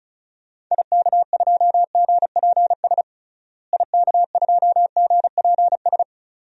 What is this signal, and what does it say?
Morse code, heard as a keyed tone, about 36 wpm, 700 Hz. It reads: IK2GPS IK2GPS
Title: DXCC-100 Call Signs - 6 Characters - Repeated 2x - 35wpm